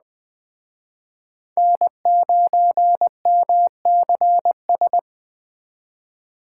Morse code, heard as a keyed tone, about 21 wpm, 700 Hz. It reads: N9MCS